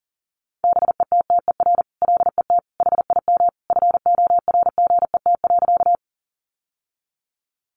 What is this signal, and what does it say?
Morse code, heard as a keyed tone, about 40 wpm, 700 Hz. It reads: BETTER LET HIM FORGET.
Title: Sentences from Top 1200 Words 40wpm